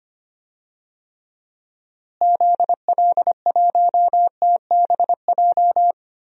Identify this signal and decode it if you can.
Morse code, heard as a keyed tone, about 25 wpm, 700 Hz. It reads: ZL1TBJ
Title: DXCC-100 Call Signs - 6 Characters - Repeated 2x - 25wpm